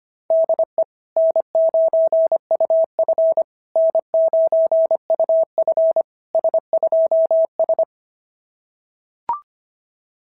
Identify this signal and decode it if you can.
Morse code, heard as a keyed tone, about 25 wpm, 690 Hz. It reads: DE N9UF N9UF S2S
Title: SOTA QSO Practice 25wpm